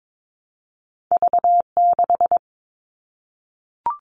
Morse code, heard as a keyed tone, about 22 wpm, 700 Hz. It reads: V6
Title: All Cut Numbers - 2-Digit 22wpm